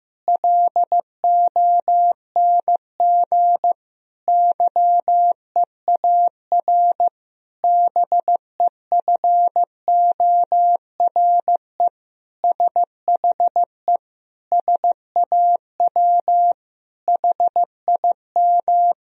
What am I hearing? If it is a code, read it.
Morse code, heard as a keyed tone, about 15 wpm, 700 Hz. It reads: LONG YEAR BEFORE SHE SAW HIM
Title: Sentences from Top 1000 Words - Review Part 1 - 15wpm